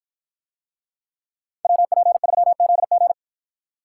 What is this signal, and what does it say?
Morse code, heard as a keyed tone, about 50 wpm, 700 Hz. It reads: WP3ZG